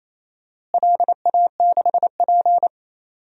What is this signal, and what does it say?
Morse code, heard as a keyed tone, about 28 wpm, 700 Hz. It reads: LA6P
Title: DXCC-100 Call Signs - 4 Characters - Repeated 2x - 28wpm